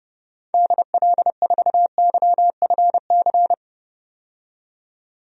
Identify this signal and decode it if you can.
Morse code, heard as a keyed tone, about 30 wpm, 700 Hz. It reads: DL4YFC
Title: DXCC-100 Call Signs - 6 Characters - 30wpm